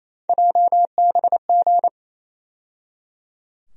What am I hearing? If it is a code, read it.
Morse code, heard as a keyed tone, about 28 wpm, 700 Hz. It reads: JBG
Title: USA Call Sign Postfixes - 3 Letters 28wpm